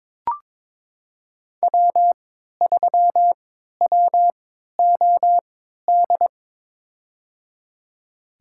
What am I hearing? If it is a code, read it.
Morse code, heard as a keyed tone, about 22 wpm, spaced as for 13 wpm, 700 Hz. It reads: W3WOD
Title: USA Call Signs - 5 Characters - 3x Character Spacing - 22wpm